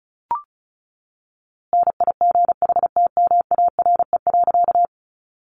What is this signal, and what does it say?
Morse code, heard as a keyed tone, about 34 wpm, 700 Hz. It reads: NIGHTMARE.
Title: A Sound Of Thunder 35wpm